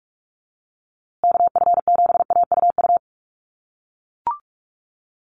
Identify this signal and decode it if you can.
Morse code, heard as a keyed tone, about 44 wpm, 700 Hz. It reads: KF7AUU
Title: USA Call Signs Repeated 3x 45wpm